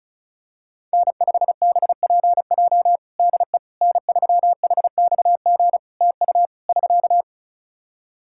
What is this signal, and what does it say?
Morse code, heard as a keyed tone, about 34 wpm, 700 Hz. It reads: N5BPJ DE N3HXG TU <SK>
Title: K1USN SST QSO Practice 35wpm